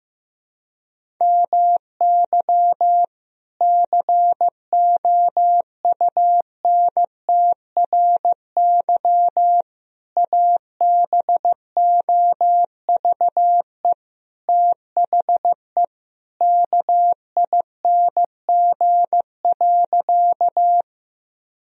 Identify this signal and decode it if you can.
Morse code, heard as a keyed tone, about 15 wpm, 700 Hz. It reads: MY COUNTRY ABOVE THE KING.
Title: Sentences from Top 1000 Words - Review Part 2 - 15wpm